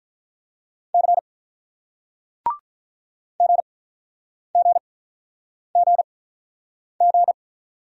Morse code, heard as a keyed tone, about 47 wpm, spaced as for 9 wpm, 700 Hz. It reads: C GGGG